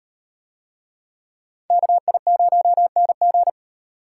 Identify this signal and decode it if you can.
Morse code, heard as a keyed tone, about 36 wpm, 700 Hz. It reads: KI0NG